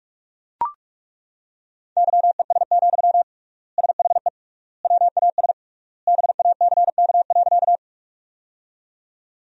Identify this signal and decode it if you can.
Morse code, heard as a keyed tone, about 41 wpm, 700 Hz. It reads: YES, SHE WAS BACK.
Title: 3x Word-Spacing - Sentences from Top 500 Words - 45wpm